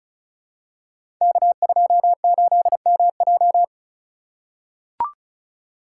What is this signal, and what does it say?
Morse code, heard as a keyed tone, about 35 wpm, 700 Hz. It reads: K28MJ
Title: Code Groups - 5 Letter:Numbers - 35wpm